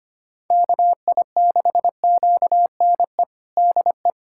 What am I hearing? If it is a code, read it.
Morse code, heard as a keyed tone, about 25 wpm, 700 Hz. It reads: KI6QNE DE
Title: USA Callsign DE Callsign 25wpm